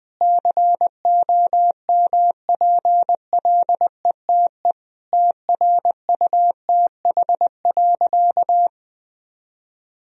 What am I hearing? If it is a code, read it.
Morse code, heard as a keyed tone, about 20 wpm, 700 Hz. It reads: COMPLETE TRUTH.
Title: Sentences from Top 1000 Words - Review Part 2 - 20wpm